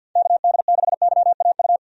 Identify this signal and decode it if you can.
Morse code, heard as a keyed tone, about 52 wpm, 700 Hz. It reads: KD6YAU
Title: USA Call Signs - 6 Characters - 50wpm